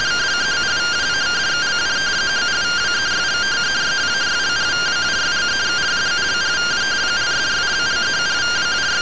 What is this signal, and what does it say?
Two-tone teleprinter signal, about 170 Hz shift.
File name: findings / MR AXIAL 33,99mm 1.gen-22 rtty rx